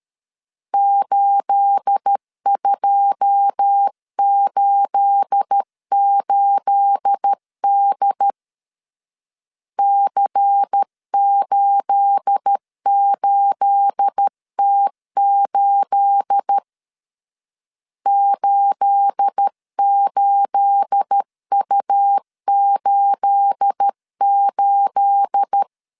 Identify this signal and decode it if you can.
Morse code, heard as a keyed tone, about 13 wpm, 800 Hz. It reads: 8288D C88T8 88U88
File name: LEZIONI / 9 28 12wpm 2351